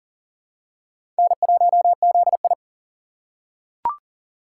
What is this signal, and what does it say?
Morse code, heard as a keyed tone, about 40 wpm, 700 Hz. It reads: N1ZI
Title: USA Call Signs - 4 Characters - 3x Character Spacing - 40wpm